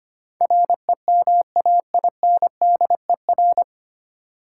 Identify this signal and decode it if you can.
Morse code, heard as a keyed tone, about 25 wpm, 700 Hz. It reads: REMAINDER
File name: practice / 25wpm / 9 Letter Words - First 3 Letters Given - 25wpm